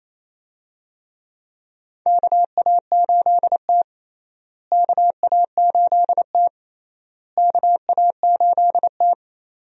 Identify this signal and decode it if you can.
Morse code, heard as a keyed tone, about 28 wpm, 700 Hz. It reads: KA8T KA8T KA8T
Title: USA Call Signs - 4 Characters - Repeated 3x - 28wpm